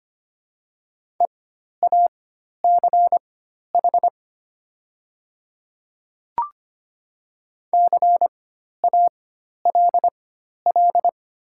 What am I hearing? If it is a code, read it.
Morse code, heard as a keyed tone, about 25 wpm, spaced as for 12 wpm, 700 Hz. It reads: EACH CALL
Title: 4x Character Spacing - Top 200 Words - 25wpm